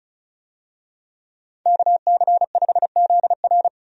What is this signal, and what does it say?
Morse code, heard as a keyed tone, about 36 wpm, 700 Hz. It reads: KC5ZR
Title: USA Call Signs - 5 Characters - 35wpm